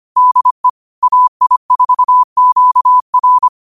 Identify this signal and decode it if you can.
Morse code, heard as a keyed tone, about 25 wpm, 1000 Hz. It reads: DE AI4QR
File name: qotdfiles / 25wpm